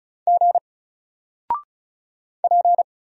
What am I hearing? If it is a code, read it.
Morse code, heard as a keyed tone, about 36 wpm, 700 Hz. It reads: G P
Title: Single Letter-Number 35wpm